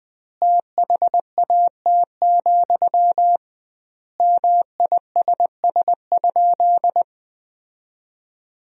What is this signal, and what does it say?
Morse code, heard as a keyed tone, about 20 wpm, 700 Hz. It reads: THAT, MISS?
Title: 2x Word-Spacing - Sentences from Top 500 Words - 20wpm